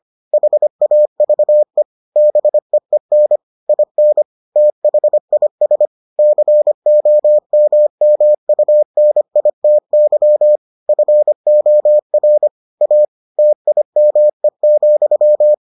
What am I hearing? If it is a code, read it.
Morse code, heard as a keyed tone, about 25 wpm, 600 Hz. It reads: HAVE BEEN IN THIS COMMUNITY FOR A TIME,